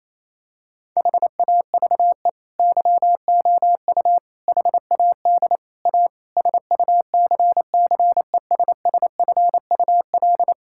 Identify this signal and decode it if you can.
Morse code, heard as a keyed tone, about 28 wpm, 690 Hz. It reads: HAVE YOU HAD A SUCCESSFUL
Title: Sentences from Top 1000 Words - Review Part 6 - 28wpm